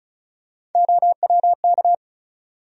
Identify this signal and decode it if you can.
Morse code, heard as a keyed tone, about 34 wpm, 700 Hz. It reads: OWK
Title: USA Call Sign Postfixes - 3 Letters 35wpm